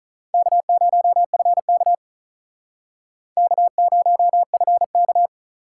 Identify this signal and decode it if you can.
Morse code, heard as a keyed tone, about 45 wpm, 700 Hz. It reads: K0FK K0FK